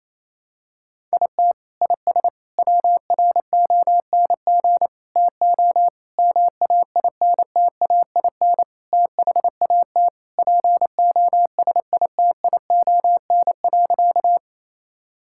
Morse code, heard as a keyed tone, about 28 wpm, 700 Hz. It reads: IT IS WRONG TO MAINTAIN THAT POSITION.
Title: Sentences from Top 1000 Words - Review Part 5 - 28wpm